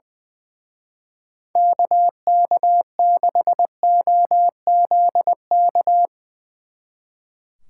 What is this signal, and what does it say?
Morse code, heard as a keyed tone, about 22 wpm, 700 Hz. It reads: KK6OZK
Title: Speed Racing - USA 2x3 Call Signs 20wpm